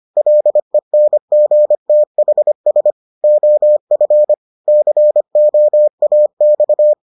Morse code, heard as a keyed tone, about 25 wpm, 600 Hz. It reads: LENGTHS OF COAX